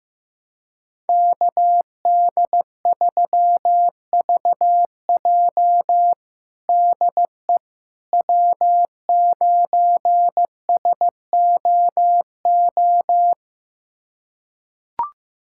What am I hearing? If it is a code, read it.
Morse code, heard as a keyed tone, about 15 wpm, 700 Hz. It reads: KD3VJ DE W9SOO
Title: USA Callsign DE Callsign - 5 Characters - 15wpm